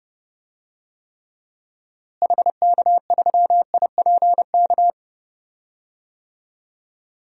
Morse code, heard as a keyed tone, about 30 wpm, 700 Hz. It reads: HK3IPK
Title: DXCC-100 Call Signs - 6 Characters - Repeated 3x - 30wpm